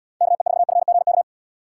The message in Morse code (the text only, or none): KE4FGK